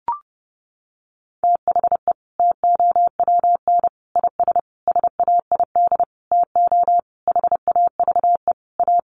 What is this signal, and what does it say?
Morse code, heard as a keyed tone, about 30 wpm, 700 Hz. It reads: THE TOWN IS SAID TO HAVE A
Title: Sentences from Top 1300 Words 30wpm